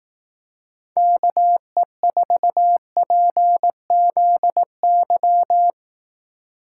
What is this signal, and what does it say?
Morse code, heard as a keyed tone, about 18 wpm, 700 Hz. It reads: KE4PZY
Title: Speed Racing - USA 2x3 Call Signs 15wpm